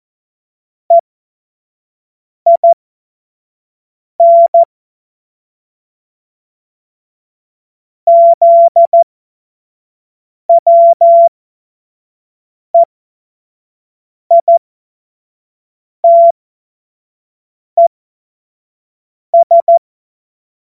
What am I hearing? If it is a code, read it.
Morse code, heard as a keyed tone, about 14 wpm, spaced as for 5 wpm, 680 Hz. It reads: EIN ZWEITES
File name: Morsekurs / Witze / Witz 02. November 2021, 25 F70